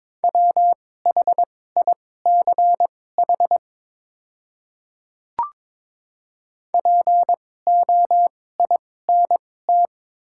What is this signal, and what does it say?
Morse code, heard as a keyed tone, about 22 wpm, spaced as for 16 wpm, 700 Hz. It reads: WHICH POINT
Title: 2x Character Spacing - Top 200 Words - 22wpm